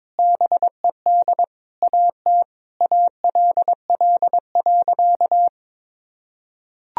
Morse code, heard as a keyed tone, about 22 wpm, 700 Hz. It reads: BED AT ALL.
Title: Sentences from Top 1000 Words - Review Part 3 - 22wpm